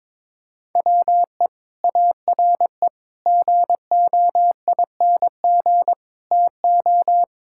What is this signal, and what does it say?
Morse code, heard as a keyed tone, about 22 wpm, 700 Hz. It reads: WE ARE GOING TO
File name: practice / 22wpm / Sentences with 4-Word Phrases - 22wpm